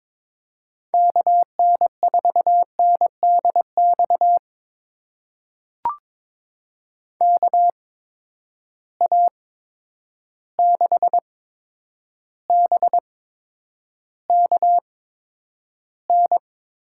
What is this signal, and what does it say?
Morse code, heard as a keyed tone, about 23 wpm, spaced as for 10 wpm, 700 Hz. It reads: KN4NDX KA6BKN
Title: USA Call Signs - 8x Character Spacing - 22wpm